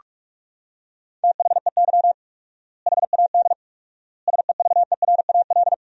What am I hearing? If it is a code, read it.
Morse code, heard as a keyed tone, about 46 wpm, 700 Hz. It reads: THEY HAD SEVERAL
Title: 4x Word-Spacing - Sentences from Top 400 Words - 45wpm